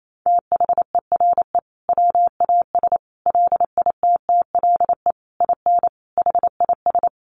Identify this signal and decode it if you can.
Morse code, heard as a keyed tone, about 28 wpm, 700 Hz. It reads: THERE WAS LITTLE IN HIS
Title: Sentences from Top 1400 Words 28wpm